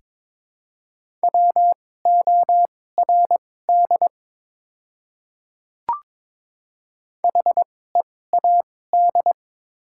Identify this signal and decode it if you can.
Morse code, heard as a keyed tone, about 22 wpm, spaced as for 16 wpm, 700 Hz. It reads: WORD HEAD
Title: 2x Character Spacing - Top 300 Words - 22wpm